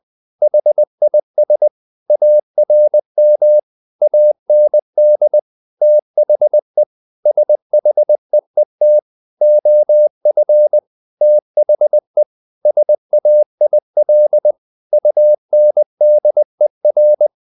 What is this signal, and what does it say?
Morse code, heard as a keyed tone, about 20 wpm, 600 Hz. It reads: HIS ARM AND THE SHEET OF THE SAIL UNDER
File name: the old man and the sea part 2 20wpm